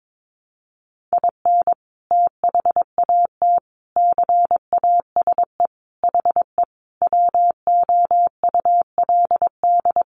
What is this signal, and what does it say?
Morse code, heard as a keyed tone, about 22 wpm, 700 Hz. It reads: IN THAT CASE HE WOULD
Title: Sentences from Top 800 Words 22wpm